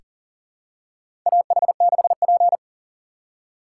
Morse code, heard as a keyed tone, about 40 wpm, 700 Hz. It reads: AH6P
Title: USA Call Signs - 4 Characters - 2x Character Spacing - 40wpm